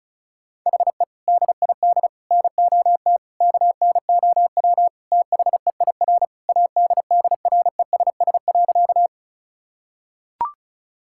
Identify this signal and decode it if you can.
Morse code, heard as a keyed tone, about 36 wpm, 700 Hz. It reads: HE DID NOT KNOW THEIR ADDRESS.
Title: Sentences from Top 1000 Words - Review Part 6 - 35wpm